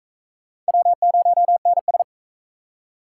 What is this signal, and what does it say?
Morse code, heard as a keyed tone, about 44 wpm, 700 Hz. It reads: W0NS